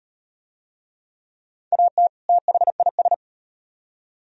Morse code, heard as a keyed tone, about 36 wpm, 700 Hz. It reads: AT THIS